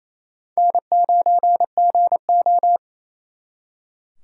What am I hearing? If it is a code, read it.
Morse code, heard as a keyed tone, about 34 wpm, 700 Hz. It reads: N9GO
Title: Speed Racing - USA Call Signs 28wpm